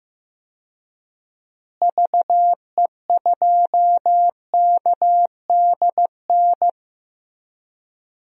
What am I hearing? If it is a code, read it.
Morse code, heard as a keyed tone, about 15 wpm, 700 Hz. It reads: VE2KDN
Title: DXCC-100 Call Signs - 6 Characters - Repeated 3x - 15wpm